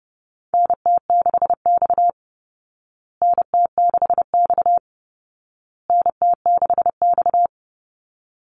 Morse code, heard as a keyed tone, about 28 wpm, 700 Hz. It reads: NT6X NT6X NT6X
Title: Call Signs Repeated 2x - 30wpm